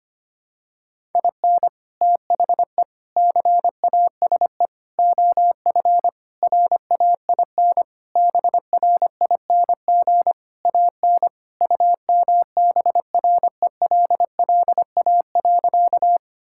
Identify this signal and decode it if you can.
Morse code, heard as a keyed tone, about 25 wpm, 700 Hz. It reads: IN THE CASE OF RAIN BRING AN UMBRELLA.
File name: practice / 25wpm / Sentences with 3-Word Phrases - 25wpm